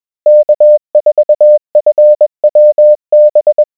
Morse code, heard as a keyed tone, about 21 wpm, 600 Hz. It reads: K4FWB
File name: morse-W4ZBB K4FWB-20wpm-600hz